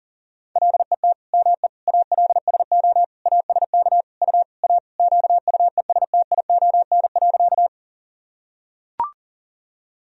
Morse code, heard as a keyed tone, about 40 wpm, 700 Hz. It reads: LET ME ALSO ASK U A QUESTION.
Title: Sentences from Top 400 Words and Abbreviations 40wpm